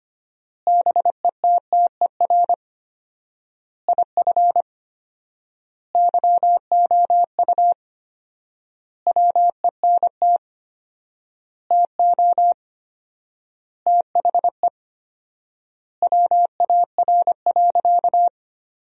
Morse code, heard as a keyed tone, about 25 wpm, 700 Hz. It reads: BETTER IF YOU WENT TO THE WAR.
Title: 4x Word-Spacing - Sentences from Top 400 Words - 25wpm